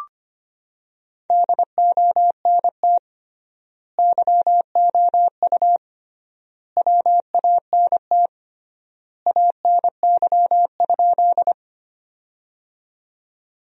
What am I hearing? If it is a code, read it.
Morse code, heard as a keyed tone, about 25 wpm, 700 Hz. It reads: DONT YOU WANT ANY?
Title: 3x Word-Spacing - Sentences from Top 300 Words - 25wpm